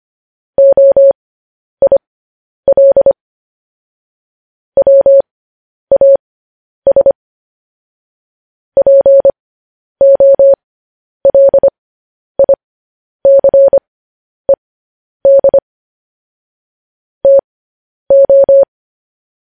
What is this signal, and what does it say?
Morse code, heard as a keyed tone, about 25 wpm, spaced as for 10 wpm, 550 Hz. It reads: OIL WAS POLICED TO
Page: news headlines in morse code at 10 wpm